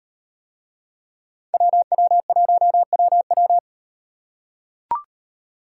Morse code, heard as a keyed tone, about 38 wpm, 700 Hz. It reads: WW1WW